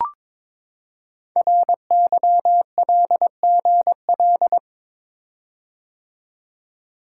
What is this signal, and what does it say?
Morse code, heard as a keyed tone, about 22 wpm, 700 Hz. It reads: RYLGL
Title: Code Groups - 5 Letters - 22wpm